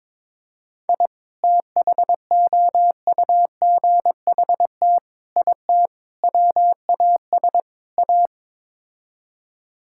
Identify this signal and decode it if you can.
Morse code, heard as a keyed tone, about 22 wpm, 700 Hz. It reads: I THOUGHT IT WAS A
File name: practice / 22wpm / Build A Phrase - 5-Word Phrases - 22wpm